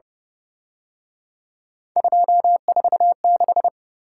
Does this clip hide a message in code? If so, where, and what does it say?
Yes: Morse code from 2.0 to 3.7 s, heard as a keyed tone, about 30 wpm, 700 Hz. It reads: 246